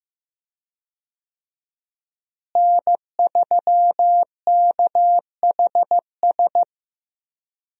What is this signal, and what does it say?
Morse code, heard as a keyed tone, about 15 wpm, 700 Hz. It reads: N3KHS
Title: USA Call Signs - 5 Characters - Repeated 3x - 15wpm